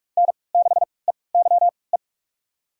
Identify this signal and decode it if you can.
Morse code, heard as a keyed tone, about 43 wpm, spaced as for 25 wpm, 700 Hz. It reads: N6EYE
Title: USA Call Signs - 5 Characters - 3x Character Spacing - 45wpm